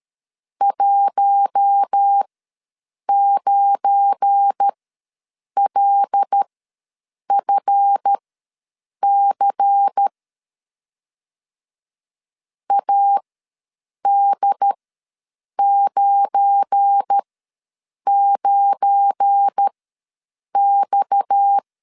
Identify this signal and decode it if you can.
Morse code, heard as a keyed tone, about 13 wpm, spaced as for 7 wpm, 800 Hz. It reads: 19LFC AD99X
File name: LEZIONI / 7 GF9Z / 7 GF9 ws 12wpm 3344